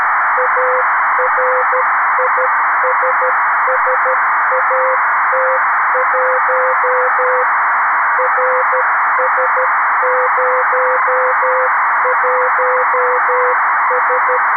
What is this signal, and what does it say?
Morse code, heard as a keyed tone, about 13 wpm, 500 Hz, with noise. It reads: ARISSAT1 RS01S
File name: arissat / recordings / ARISSat 11Sept2011 1354utc 60frames